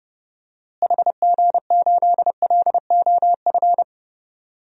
Morse code, heard as a keyed tone, about 32 wpm, 700 Hz. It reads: HG8LOF
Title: DXCC-100 Call Signs - 6 Characters - Repeated 2x - 30wpm